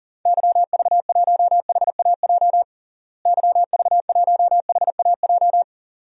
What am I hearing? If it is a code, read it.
Morse code, heard as a keyed tone, about 40 wpm, 700 Hz. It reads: YV1HAJ YV1HAJ
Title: DXCC-100 Call Signs - 6 Characters - Repeated 2x - 40wpm